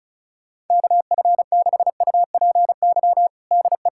Morse code, heard as a keyed tone, about 36 wpm, 690 Hz. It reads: KF6UPY DE
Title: USA Callsign DE Callsign - 6 Characters - 35wpm